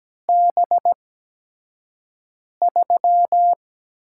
Morse code, heard as a keyed tone, about 17 wpm, spaced as for 5 wpm, 700 Hz. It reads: B3
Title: USA Call Signs - 4 Characters - 8x Character Spacing - 17wpm